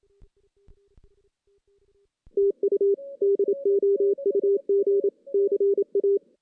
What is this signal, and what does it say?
Morse code, heard as a keyed tone, about 27 wpm, 400 Hz. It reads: TU DOUG CA
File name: 2016 JAN CW NAQP / 2016 Jan naqp k5ot - 0-5